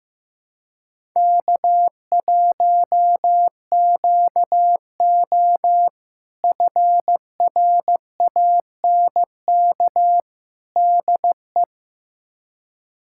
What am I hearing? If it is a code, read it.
Morse code, heard as a keyed tone, about 15 wpm, 700 Hz. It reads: K1QO FRANK DE
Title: K1USN SST QSO Practice - 15wpm